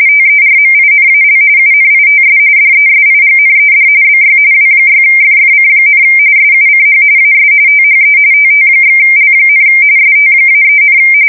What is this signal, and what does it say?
RTTY, about 45.45 baud. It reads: TERMINAL UNITS DESIGNED BY W6FFC, SUCH AS THE TT/L, ST-3,
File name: fsk170 / teletype-RTTY